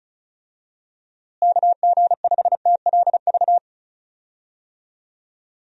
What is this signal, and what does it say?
Morse code, heard as a keyed tone, about 36 wpm, 700 Hz. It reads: KG5TLV
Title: USA Call Signs - 6 Characters - Repeated 3x - 35wpm